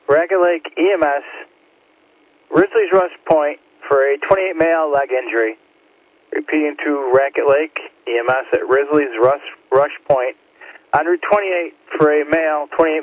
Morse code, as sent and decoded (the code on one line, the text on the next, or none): none